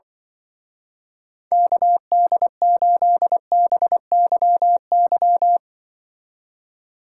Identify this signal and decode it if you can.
Morse code, heard as a keyed tone, about 24 wpm, 700 Hz. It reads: KD8BYY
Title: Speed Racing - USA 2x3 Call Signs 20wpm